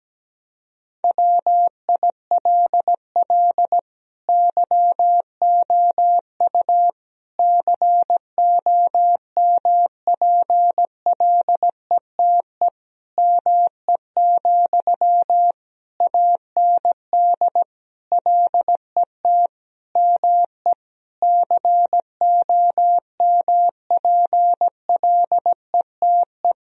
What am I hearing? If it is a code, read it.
Morse code, heard as a keyed tone, about 17 wpm, 700 Hz. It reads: WILL YOU COMPLETE ME, AND LET ME COMPLETE
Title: Sentences from Top 1000 Words - Review Part 2 - 17wpm